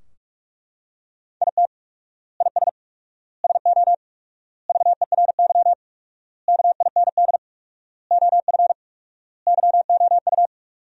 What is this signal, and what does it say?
Morse code, heard as a keyed tone, about 42 wpm, 700 Hz. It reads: IT IS SO VERY KIND OF YOU
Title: 4x Word-Spacing - Sentences from Top 300 Words - 45wpm